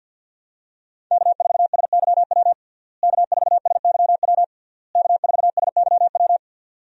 Morse code, heard as a keyed tone, about 51 wpm, 700 Hz. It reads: K4SYW K4SYW K4SYW
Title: USA Call Signs - 5 Characters - Repeated 3x - 50wpm